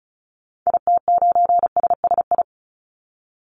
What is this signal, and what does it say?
Morse code, heard as a keyed tone, about 34 wpm, 700 Hz. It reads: IT9SSI
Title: Call Signs Repeated 3x - 35wpm